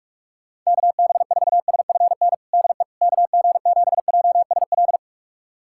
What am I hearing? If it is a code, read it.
Morse code, heard as a keyed tone, about 42 wpm, 700 Hz. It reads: KB4SFN DE KG7JSL
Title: USA Callsign DE Callsign - 6 Characters - 45wpm